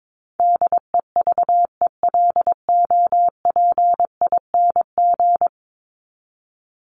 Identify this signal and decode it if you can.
Morse code, heard as a keyed tone, about 22 wpm, 700 Hz. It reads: DEVELOPING
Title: ING Suffix - Top 100 Words with Base Words 22wpm